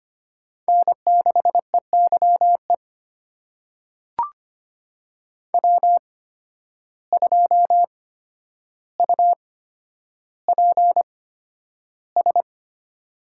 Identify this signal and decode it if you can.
Morse code, heard as a keyed tone, about 25 wpm, spaced as for 11 wpm, 700 Hz. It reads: N6EYE W2UPS
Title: USA Call Signs - 5 Characters - 8x Character Spacing - 25wpm